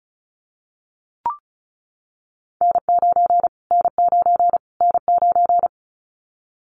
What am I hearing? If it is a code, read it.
Morse code, heard as a keyed tone, about 34 wpm, 700 Hz. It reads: N9 N9 N9
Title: Callsign Prefixes for Top 25 DXCC - Repeated 3x - 35wpm